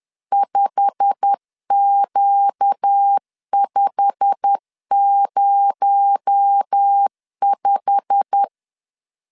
Morse code, heard as a keyed tone, about 10 wpm, 800 Hz. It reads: 5Q505
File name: LEZIONI / 8 YJ45 / 8 YJ45 10wpm 2549